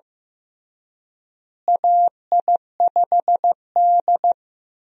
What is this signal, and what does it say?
Morse code, heard as a keyed tone, about 15 wpm, 700 Hz. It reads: AI5D
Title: USA Call Signs - 4 Characters - Repeated 3x - 15wpm